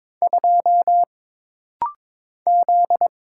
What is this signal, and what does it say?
Morse code, heard as a keyed tone, about 22 wpm, 700 Hz. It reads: 2 Z
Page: This rapid-fire timing is geared to push your ICR (Instant Character Recognition) skills.